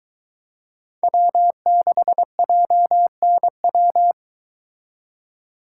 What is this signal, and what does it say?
Morse code, heard as a keyed tone, about 23 wpm, 700 Hz. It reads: W6JNW